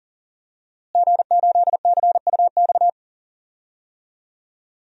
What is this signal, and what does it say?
Morse code, heard as a keyed tone, about 37 wpm, 700 Hz. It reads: G8CUX
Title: DXCC-100 Call Signs - 5 Characters - Repeated 2x - 40wpm